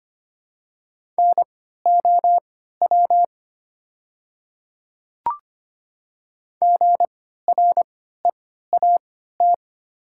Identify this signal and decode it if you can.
Morse code, heard as a keyed tone, about 25 wpm, spaced as for 14 wpm, 700 Hz. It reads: NOW GREAT
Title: 3x Character Spacing - Top 500 Words - 25wpm